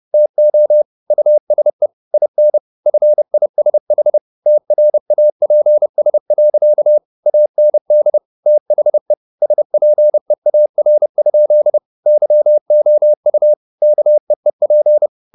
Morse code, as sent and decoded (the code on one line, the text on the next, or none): - --- / ..- ... / .. -. / ..-. .. ... .... / - .-. .- .--. ... .-.-.- / .- -. -.. / - .... / ... .--. .- .-. ..--.. / -.-- --- ..- / -.- . .--.
TO USE IN FISH TRAPS. AND THE SPEAR? YOU KEEP